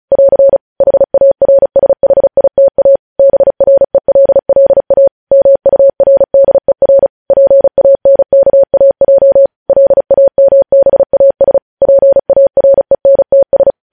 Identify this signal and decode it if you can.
Morse code, heard as a keyed tone, about 36 wpm, 550 Hz. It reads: <AR> HARSHITA BRELLA MURDER PANKAJ LAMBAS PARENTS
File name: podcast / News headlines-1x-35x35